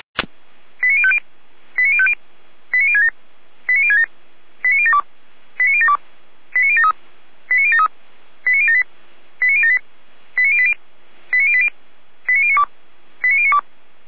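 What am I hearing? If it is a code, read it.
Five-tone selective call (ZVEI), heard as quick runs of five tones, tones of about 70 ms. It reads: 89050 89050 89067 89067 89072 89072 89073 89073 89078 89078 89080 89080 89092 89092
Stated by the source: Five-tone ZVEI selective call received on an SDR.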